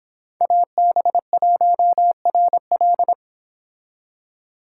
Morse code, heard as a keyed tone, about 26 wpm, 700 Hz. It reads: AB1RL